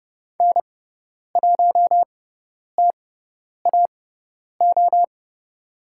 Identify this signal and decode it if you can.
Morse code, heard as a keyed tone, about 30 wpm, spaced as for 10 wpm, 700 Hz. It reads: N1TAO
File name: Lesson 16g - Letter c - Call Signs 30wpm